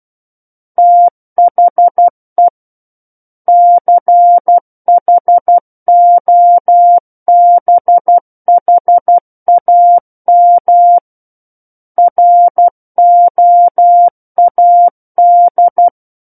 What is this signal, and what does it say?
Morse code, heard as a keyed tone, about 12 wpm, 700 Hz. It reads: THE CHOBHAM ROAD
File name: Book One / 03-Chapter 3 - 12 WPM